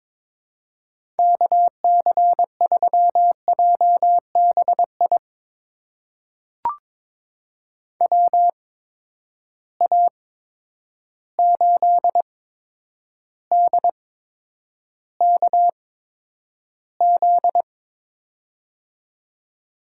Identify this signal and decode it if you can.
Morse code, heard as a keyed tone, about 22 wpm, spaced as for 9 wpm, 700 Hz. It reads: KC3JBI WA8DKZ
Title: USA Call Signs - 6 Characters - 8x Character Spacing - 22wpm